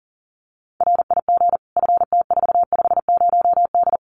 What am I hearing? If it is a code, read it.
Morse code, heard as a keyed tone, about 37 wpm, 700 Hz. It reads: RIG FT450D
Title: Common QSO Phrases 40wpm